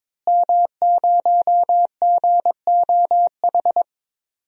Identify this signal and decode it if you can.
Morse code, heard as a keyed tone, about 22 wpm, 700 Hz. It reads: M0GOH
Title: DXCC-100 Call Signs - 5 Characters - Repeated 2x - 22wpm